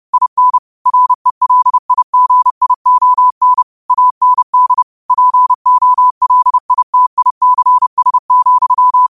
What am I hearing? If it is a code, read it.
Morse code, heard as a keyed tone, about 30 wpm, 1000 Hz. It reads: IN RELIGION AND POLITICS,